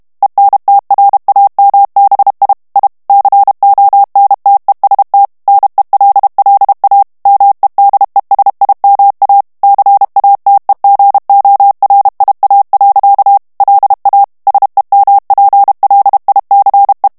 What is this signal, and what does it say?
Morse code, heard as a keyed tone, about 32 wpm, 800 Hz. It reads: ENTRAMBI I CONTEST NELLA MEDESIMA CATEGORIA. LA SEMPLICE
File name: ITALIEN texte clair 2 vitesse 30 mots minute